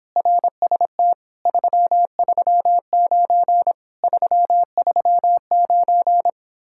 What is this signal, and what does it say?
Morse code, heard as a keyed tone, about 26 wpm, 700 Hz. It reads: RST 339 339